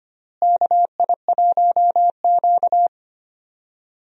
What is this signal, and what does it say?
Morse code, heard as a keyed tone, about 25 wpm, 700 Hz. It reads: KI1Q
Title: USA Call Signs - 4 Characters - Repeated 2x - 25wpm